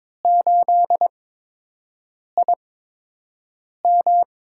Morse code, heard as a keyed tone, about 22 wpm, spaced as for 6 wpm, 700 Hz. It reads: 8IM
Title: USA Call Signs - 5 Characters - 8x Character Spacing - 22wpm